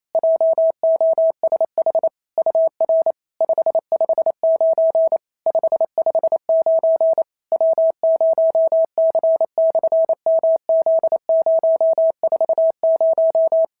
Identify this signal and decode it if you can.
Morse code, heard as a keyed tone, about 28 wpm, 690 Hz. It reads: JOSH UR 559 559 W0C/MZ040
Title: SOTA QSO Practice 28wpm